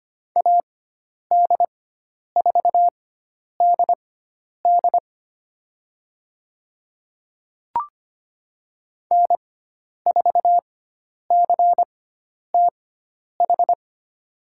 Morse code, heard as a keyed tone, about 25 wpm, spaced as for 10 wpm, 700 Hz. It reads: AD4DD N4CTH
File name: Lesson 16g - Letter c - Call Signs 25wpm